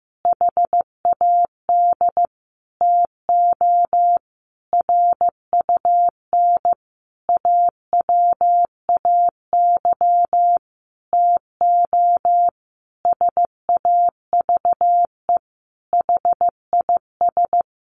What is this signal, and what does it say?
Morse code, heard as a keyed tone, about 15 wpm, 700 Hz. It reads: HAD TO RUN AWAY TO SAVE HIS
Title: Sentences from Top 1100 Words 15wpm